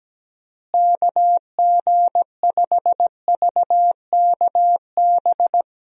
Morse code, heard as a keyed tone, about 17 wpm, 700 Hz. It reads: KG5VKB
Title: USA Call Signs - 6 Characters - Repeated 3x - 17wpm